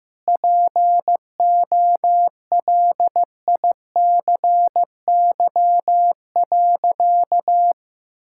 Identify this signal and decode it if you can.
Morse code, heard as a keyed tone, about 15 wpm, 700 Hz. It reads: POLICY.
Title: Sentences from Top 1000 Words - Review Part 5 - 15wpm